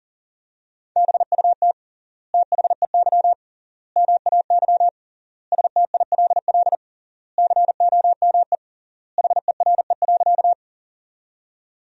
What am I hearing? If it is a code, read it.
Morse code, heard as a keyed tone, about 40 wpm, 700 Hz. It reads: BUT THEY MAY STILL COME HERE.
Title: 3x Word-Spacing - Sentences from Top 300 Words - 40wpm